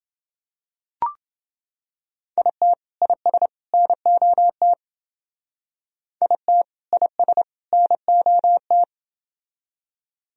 Morse code, heard as a keyed tone, about 29 wpm, 700 Hz. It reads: IT IS NOT IT IS NOT